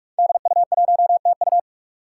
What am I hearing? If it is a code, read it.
Morse code, heard as a keyed tone, about 44 wpm, 700 Hz. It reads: DU1TU
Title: Five Challenging Letter-Number Combinations 45wpm